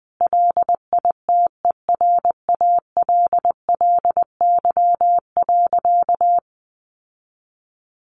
Morse code, heard as a keyed tone, about 20 wpm, 700 Hz. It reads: LITERALLY.
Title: Sentences from Top 1400 Words 20wpm